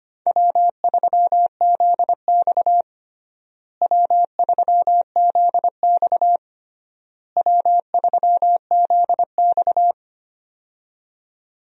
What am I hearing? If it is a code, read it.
Morse code, heard as a keyed tone, about 25 wpm, 700 Hz. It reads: W3ZX W3ZX W3ZX
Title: USA Call Signs - 4 Characters - Repeated 3x - 25wpm